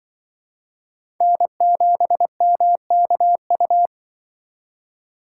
Morse code, heard as a keyed tone, about 24 wpm, 700 Hz. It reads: N7MKU
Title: Speed Racing - USA 1x3 Call Signs 20wpm